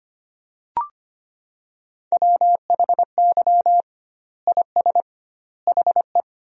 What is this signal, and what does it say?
Morse code, heard as a keyed tone, about 25 wpm, 700 Hz. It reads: WHY IS HE
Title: 2x Word-Spacing - Sentences from Top 300 Words - 25wpm